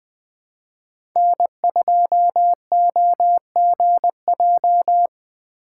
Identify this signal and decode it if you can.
Morse code, heard as a keyed tone, about 21 wpm, 700 Hz. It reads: N2OGJ
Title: USA Call Signs - 3x Character Spacing - 20wpm